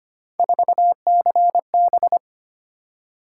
Morse code, heard as a keyed tone, about 25 wpm, 700 Hz. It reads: 4CB
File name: Three Letter-Number Combinations - 25wpm